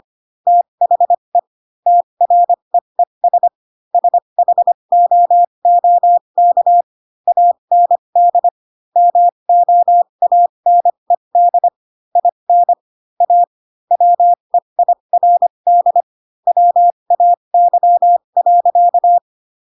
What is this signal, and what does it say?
Morse code, heard as a keyed tone, about 25 wpm, spaced as for 22 wpm, 700 Hz. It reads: THE TREES SHOOK AND MOANED IN A WEIRD WAY.
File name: Pinocchio / Pinocchio ch6 18 25 ik1hge